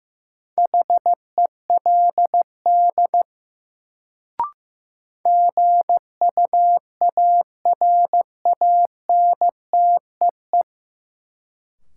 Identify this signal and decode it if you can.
Morse code, heard as a keyed tone, about 15 wpm, 700 Hz. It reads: HELD GUARANTEE
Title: Sentences from Top 1600 Words - New Word Drill - 15wpm